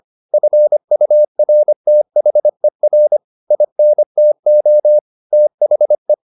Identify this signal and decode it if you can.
Morse code, heard as a keyed tone, about 25 wpm, 600 Hz. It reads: FURTHER INTO THE